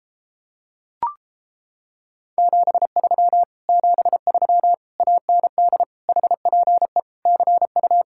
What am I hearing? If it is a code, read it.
Morse code, heard as a keyed tone, about 29 wpm, 700 Hz. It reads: 73 73 AND HPE CU